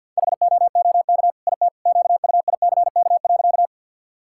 Morse code, heard as a keyed tone, about 50 wpm, 700 Hz. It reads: HOOK IT QUICK.
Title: Sentences from Top 1600 Words 50wpm